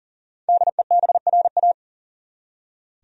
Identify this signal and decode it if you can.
Morse code, heard as a keyed tone, about 39 wpm, 700 Hz. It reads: DEBRA
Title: USA Common Names - 40wpm